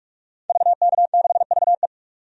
Morse code, heard as a keyed tone, about 46 wpm, 700 Hz. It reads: VK6VE
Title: DXCC-100 Call Signs - 5 Characters - 45wpm